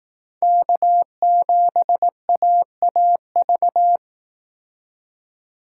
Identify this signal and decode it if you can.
Morse code, heard as a keyed tone, about 19 wpm, 700 Hz. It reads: K7AAV
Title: Speed Racing - USA 1x3 Call Signs 15wpm